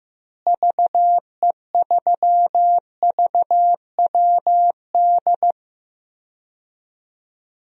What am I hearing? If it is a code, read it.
Morse code, heard as a keyed tone, about 15 wpm, 700 Hz. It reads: VE3VWD
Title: DXCC-100 Call Signs - 6 Characters - Repeated 3x - 15wpm